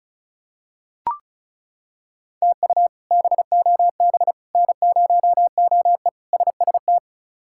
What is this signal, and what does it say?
Morse code, heard as a keyed tone, about 34 wpm, 700 Hz. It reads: TU BOB N0OE SST
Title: K1USN SST QSO Practice - 35wpm